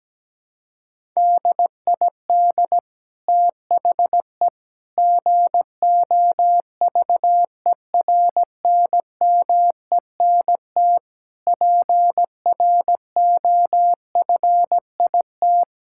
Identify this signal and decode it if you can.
Morse code, heard as a keyed tone, about 17 wpm, 700 Hz. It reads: DID THE GOVERNMENT PROFIT
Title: Sentences from Top 1000 Words - Review Part 5 - 17wpm